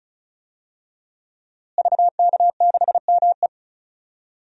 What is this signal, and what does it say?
Morse code, heard as a keyed tone, about 34 wpm, 700 Hz. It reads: VK6ME
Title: DXCC-100 Call Signs - 5 Characters - Repeated 2x - 35wpm